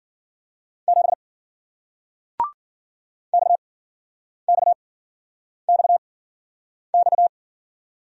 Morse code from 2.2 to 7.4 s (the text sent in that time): XXXX